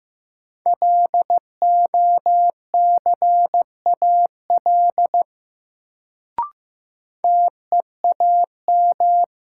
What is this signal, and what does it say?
Morse code, heard as a keyed tone, about 15 wpm, 700 Hz. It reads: LOCAL TEAM
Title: Sentences from Top 1500 Words - New Word Drill Alt - 15wpm